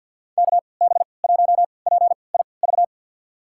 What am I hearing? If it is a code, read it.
Morse code, heard as a keyed tone, about 52 wpm, spaced as for 29 wpm, 700 Hz. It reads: KB1PIV
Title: USA Call Signs - 6 Characters - 3x Character Spacing - 50wpm